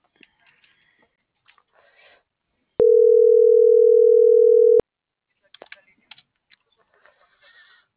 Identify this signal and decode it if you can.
Ringback tone (North America), 440 + 480 Hz.